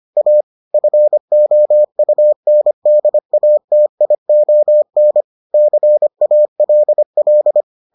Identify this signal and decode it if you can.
Morse code, heard as a keyed tone, about 25 wpm, 600 Hz. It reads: A FOUNDATION CALL